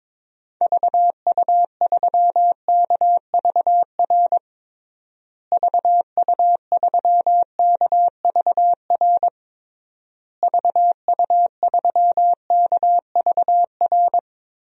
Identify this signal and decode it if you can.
Morse code, heard as a keyed tone, about 22 wpm, 700 Hz. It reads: VU3KVR VU3KVR VU3KVR
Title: DXCC-100 Call Signs - 6 Characters - Repeated 3x - 22wpm